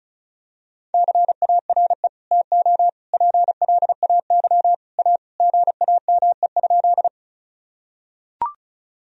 Morse code, heard as a keyed tone, about 36 wpm, 700 Hz. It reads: CARE TO PLAY A GAME?
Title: Sentences from Top 1000 Words - Review Part 1 - 35wpm